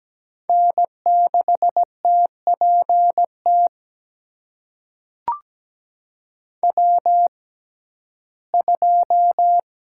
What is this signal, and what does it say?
Morse code, heard as a keyed tone, about 17 wpm, spaced as for 12 wpm, 700 Hz. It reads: N6TPT W2
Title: USA Call Signs - 5 Characters - 6x Character Spacing - 17wpm